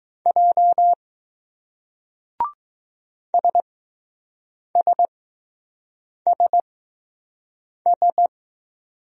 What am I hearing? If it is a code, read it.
Morse code, heard as a keyed tone, about 20 wpm, spaced as for 6 wpm, 700 Hz. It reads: J SSSS